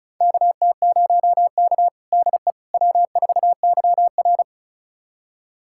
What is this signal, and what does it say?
Morse code, heard as a keyed tone, about 36 wpm, 700 Hz. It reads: KT0K DE W4YR
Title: USA Callsign DE Callsign - 4 Characters - 35wpm